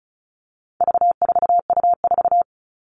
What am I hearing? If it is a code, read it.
Morse code, heard as a keyed tone, about 34 wpm, 700 Hz. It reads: V4U4
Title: All Cut Numbers - 4-Digit 35wpm